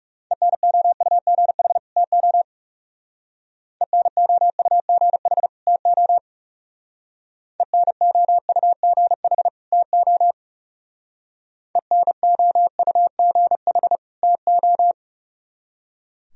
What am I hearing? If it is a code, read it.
Morse code, heard as a keyed tone, about 35 wpm, 700 Hz. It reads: ENOUGH TO ENOUGH TO ENOUGH TO ENOUGH TO
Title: Speed-Racing - Sets of 2 Words 30wpm